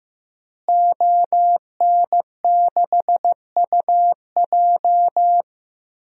Morse code, heard as a keyed tone, about 15 wpm, 700 Hz. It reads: ON6UJ
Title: DXCC-100 Call Signs - 5 Characters - Repeated 3x - 15wpm